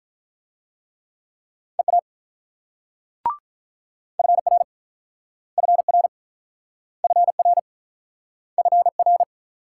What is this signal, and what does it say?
Morse code, heard as a keyed tone, about 48 wpm, 700 Hz. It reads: EA FR FR FR FR